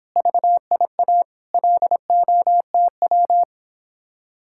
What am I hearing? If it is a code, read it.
Morse code, heard as a keyed tone, about 31 wpm, 700 Hz. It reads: VIA LOTW